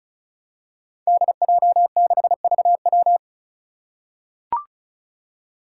Morse code, heard as a keyed tone, about 34 wpm, 700 Hz. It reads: DJ6VW
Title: Five Challenging Letter-Number Combinations 35wpm